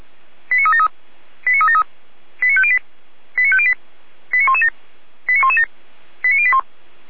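Five-tone selective call (ZVEI), heard as quick runs of five tones, tones of about 70 ms. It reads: 89373 89373 89508 89508 89117 89117 89081
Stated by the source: ZVEI five-tone selective call 89080 received by SDR.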